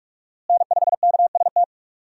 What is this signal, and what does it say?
Morse code, heard as a keyed tone, about 47 wpm, 700 Hz. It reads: N5KST
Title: USA Call Signs - 5 Characters - Repeated 2x - 45wpm